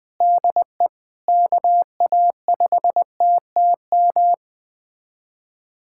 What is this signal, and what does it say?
Morse code, heard as a keyed tone, about 20 wpm, 700 Hz. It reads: DE KA5TTM
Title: USA Callsign DE Callsign 20wpm